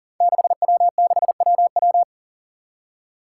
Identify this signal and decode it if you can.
Morse code, heard as a keyed tone, about 43 wpm, 700 Hz. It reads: 6W6WW
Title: Five Challenging Letter-Number Combinations 40wpm